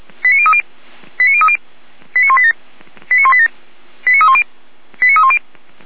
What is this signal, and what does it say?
Five-tone selective call (ZVEI), heard as quick runs of five tones, tones of about 70 ms. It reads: 89030 89030 89187 89187 89310 89310